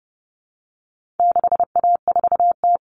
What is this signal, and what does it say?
Morse code, heard as a keyed tone, about 30 wpm, 700 Hz. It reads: 6A4T
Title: Common Cut Numbers - 4-Digit 30wpm